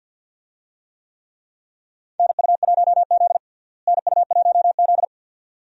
Morse code, heard as a keyed tone, about 52 wpm, 700 Hz. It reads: NU1Z NU1Z
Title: USA Call Signs - 4 Characters - Repeated 2x - 50wpm